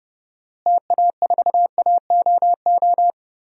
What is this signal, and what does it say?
Morse code, heard as a keyed tone, about 32 wpm, 700 Hz. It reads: TA4AOO
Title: DXCC-100 Call Signs - 6 Characters - Repeated 3x - 30wpm